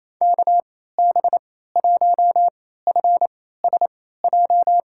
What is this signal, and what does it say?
Morse code, heard as a keyed tone, about 28 wpm, spaced as for 16 wpm, 700 Hz. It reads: KB1FSJ
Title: USA Call Signs - 6 Characters - 3x Character Spacing - 28wpm